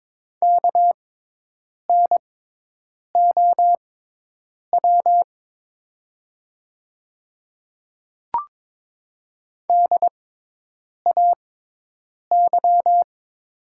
Morse code, heard as a keyed tone, about 22 wpm, spaced as for 8 wpm, 700 Hz. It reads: KNOW DAY
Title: 6x Character Spacing - Top 100 Words - 22wpm